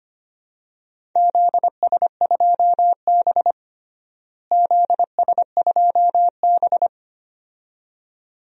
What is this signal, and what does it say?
Morse code, heard as a keyed tone, about 25 wpm, 700 Hz. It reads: ZS2B ZS2B
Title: DXCC-100 Call Signs - 4 Characters - Repeated 2x - 25wpm